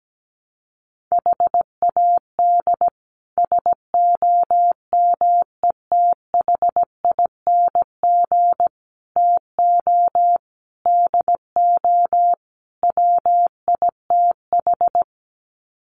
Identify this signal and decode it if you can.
Morse code, heard as a keyed tone, about 17 wpm, 700 Hz. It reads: HAD SOMETHING TO DO WITH
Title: Sets of 5 Words 17wpm